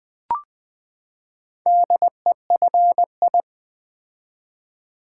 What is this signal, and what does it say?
Morse code, heard as a keyed tone, about 20 wpm, 700 Hz. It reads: DEFI
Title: Build A Word - 4 Letters At A Time - 8-Letter Words - 20wpm